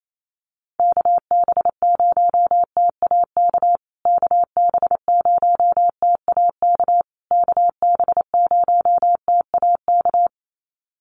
Morse code, heard as a keyed tone, about 28 wpm, 700 Hz. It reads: KB0TAK KB0TAK KB0TAK
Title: USA Call Signs Repeated 3x 28wpm